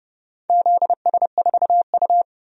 Morse code, heard as a keyed tone, about 29 wpm, 700 Hz. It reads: ZS4U
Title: DXCC-100 Call Signs - 4 Characters - 30wpm